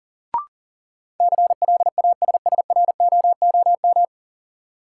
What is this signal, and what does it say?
Morse code, heard as a keyed tone, about 40 wpm, 700 Hz. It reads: CLASSROOM
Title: Sentences from Top 1500 Words - New Word Drill - 40wpm